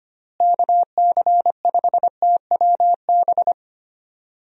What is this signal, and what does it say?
Morse code, heard as a keyed tone, about 25 wpm, 700 Hz. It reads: KC5TWB
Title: USA Call Signs - 6 Characters - Repeated 2x - 25wpm